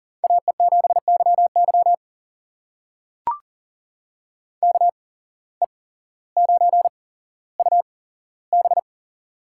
Morse code, heard as a keyed tone, about 40 wpm, spaced as for 17 wpm, 700 Hz. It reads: AE7YY KE9UB
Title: USA Call Signs - 5 Characters - 8x Character Spacing - 40wpm